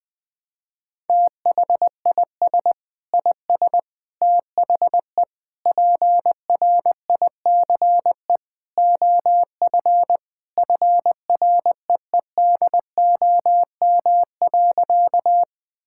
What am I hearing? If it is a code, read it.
Morse code, heard as a keyed tone, about 20 wpm, 700 Hz. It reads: THIS IS THE PRICE OF FREEDOM.
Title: Sentences from Top 1000 Words - Review Part 5 - 20wpm